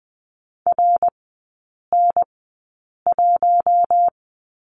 Morse code, heard as a keyed tone, about 20 wpm, spaced as for 8 wpm, 700 Hz. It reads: RN1
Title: 127 Lesson 21e - Number 6 - Three Characters 20wpm